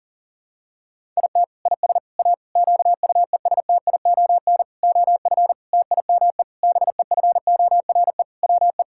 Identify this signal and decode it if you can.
Morse code, heard as a keyed tone, about 40 wpm, 690 Hz. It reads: IT IS A QUESTION OF TIME BEFORE WE
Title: Sentences from Top 1000 Words - Review Part 6 - 40wpm